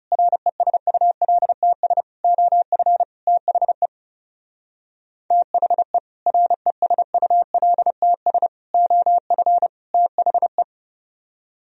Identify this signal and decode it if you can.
Morse code, heard as a keyed tone, about 37 wpm, 700 Hz. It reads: RESULTS OF THE THE RESULTS OF THE
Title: Speed-Racing - Sets of 4 Words 30wpm